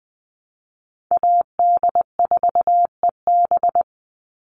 Morse code, heard as a keyed tone, about 20 wpm, 700 Hz. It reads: AD4EB
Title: Call Signs - 20wpm